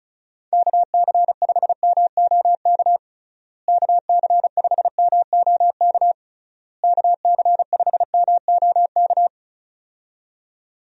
Morse code, heard as a keyed tone, about 34 wpm, 700 Hz. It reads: KC5MOK KC5MOK KC5MOK
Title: USA Call Signs - 6 Characters - Repeated 3x - 35wpm